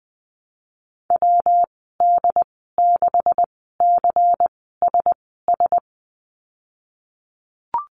Morse code, heard as a keyed tone, about 20 wpm, spaced as for 15 wpm, 700 Hz. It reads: WD6CSS
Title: 147 Lesson 24g - Letter y - Call Signs 20wpm